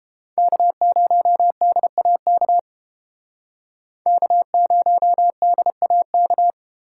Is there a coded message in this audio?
Yes: Morse code from 0.4 to 6.5 s, heard as a keyed tone, about 29 wpm, 700 Hz. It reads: K0DAK K0DAK